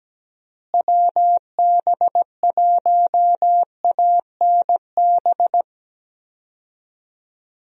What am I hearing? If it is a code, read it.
Morse code, heard as a keyed tone, about 17 wpm, 700 Hz. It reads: WB1ANB
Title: USA Call Signs - 6 Characters - Repeated 3x - 17wpm